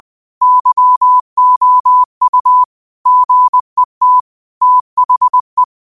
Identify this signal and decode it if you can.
Morse code, heard as a keyed tone, about 20 wpm, 1000 Hz. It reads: YOU GET THE